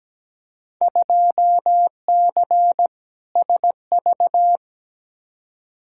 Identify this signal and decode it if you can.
Morse code, heard as a keyed tone, about 17 wpm, 700 Hz. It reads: 2C SV
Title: Field Day Exchanges Encore - 17wpm